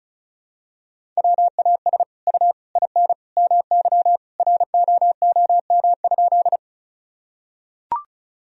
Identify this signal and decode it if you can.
Morse code, heard as a keyed tone, about 36 wpm, 700 Hz. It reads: WAS U IN MY ROOM?
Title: Sentences from Top 300 Words and Abbreviations 35wpm